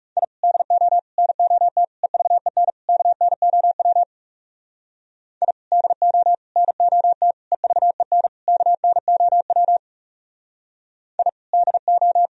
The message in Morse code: I DO NOT EVEN KNOW I DO NOT EVEN KNOW I DO

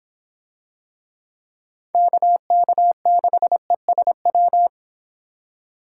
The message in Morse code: KK6ESW

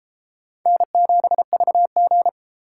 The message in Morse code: N7VG

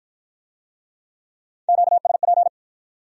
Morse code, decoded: QSP

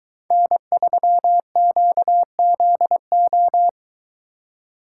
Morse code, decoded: N3QZO

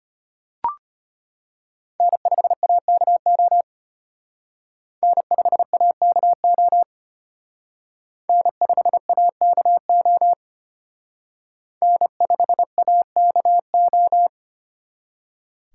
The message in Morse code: N5AKO N5AKO N5AKO N5AKO

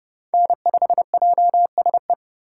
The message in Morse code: N5JSE

